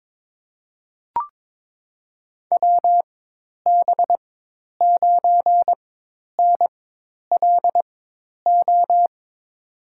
Morse code, heard as a keyed tone, about 22 wpm, spaced as for 10 wpm, 700 Hz. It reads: WB9NLO
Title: USA Call Signs - 6 Characters - 4x Character Spacing - 22wpm